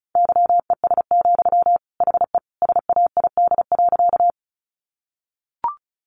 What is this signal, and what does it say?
Morse code, heard as a keyed tone, about 34 wpm, 700 Hz. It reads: YES, HE SAID.
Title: Sentences from Top 500 Words 35wpm